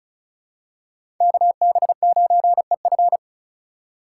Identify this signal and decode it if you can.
Morse code, heard as a keyed tone, about 36 wpm, 700 Hz. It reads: KB9EF
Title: USA Call Signs - 2x Character Spacing - 35wpm